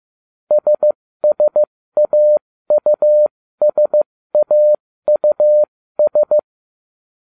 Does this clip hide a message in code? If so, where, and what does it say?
Yes: Morse code from 0.5 to 6.4 s, heard as a keyed tone, about 15 wpm, spaced as for 13 wpm, 600 Hz. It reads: SSAUSAUS